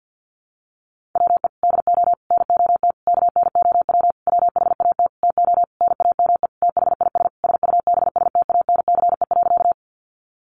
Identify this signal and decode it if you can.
Morse code, heard as a keyed tone, about 50 wpm, 700 Hz. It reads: WE DO NOT KNOW WHAT TO NAME THIS SUBSTANCE.
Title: Sentences from Top 1500 Words 50wpm